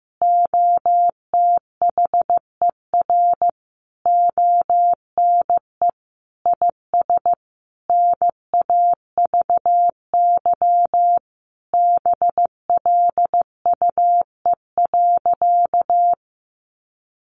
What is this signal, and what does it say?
Morse code, heard as a keyed tone, about 15 wpm, 700 Hz. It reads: OTHER ONE IS NAVY BLUE.
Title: Additional Easy Phrases 15wpm